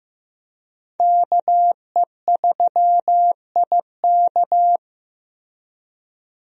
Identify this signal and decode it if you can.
Morse code, heard as a keyed tone, about 15 wpm, 700 Hz. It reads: KE3IK
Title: USA Call Signs - 5 Characters - Repeated 3x - 15wpm